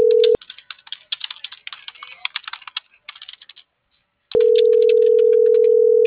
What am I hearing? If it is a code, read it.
Ringback tone (North America), 440 + 480 Hz.
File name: CENTRAL FRALSE / out abigailp.fralse